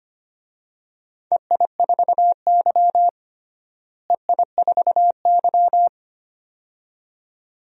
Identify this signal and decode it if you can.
Morse code, heard as a keyed tone, about 25 wpm, 700 Hz. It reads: EI4Y EI4Y
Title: DXCC-100 Call Signs - 4 Characters - Repeated 2x - 25wpm